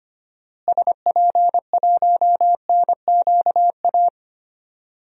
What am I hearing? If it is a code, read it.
Morse code, heard as a keyed tone, about 25 wpm, 700 Hz. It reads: SP1NQA
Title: DXCC-100 Call Signs - 6 Characters - Repeated 3x - 25wpm